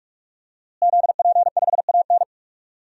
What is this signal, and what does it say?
Morse code, heard as a keyed tone, about 46 wpm, 700 Hz. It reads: ZP5AN
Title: DXCC-100 Call Signs - 5 Characters - Repeated 3x - 45wpm